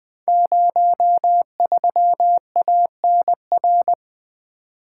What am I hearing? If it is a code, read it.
Morse code, heard as a keyed tone, about 20 wpm, 700 Hz. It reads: 03ANR
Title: Code Groups - 5 Letter:Numbers - 20wpm